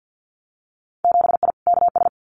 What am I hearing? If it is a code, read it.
Morse code, heard as a keyed tone, about 50 wpm, 700 Hz. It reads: 7I KS
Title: Winter Field Day Exchanges Encore 50wpm